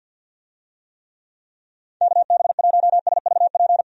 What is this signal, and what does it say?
Morse code, heard as a keyed tone, about 52 wpm, 700 Hz. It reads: KB1SVP
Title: USA Call Signs - 6 Characters - 50wpm